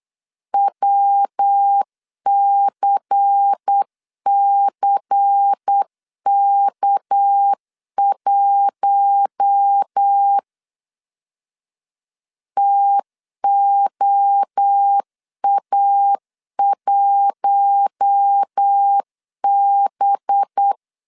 Morse code, heard as a keyed tone, about 8 wpm, 800 Hz. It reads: WCCK1 TOA1B
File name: 8wpm